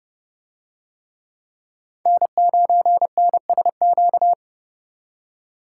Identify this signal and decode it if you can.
Morse code, heard as a keyed tone, about 27 wpm, 700 Hz. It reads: N9NSQ